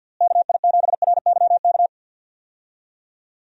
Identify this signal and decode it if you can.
Morse code, heard as a keyed tone, about 51 wpm, 700 Hz. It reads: KI7RYK
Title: USA Call Signs - 6 Characters - Repeated 2x - 50wpm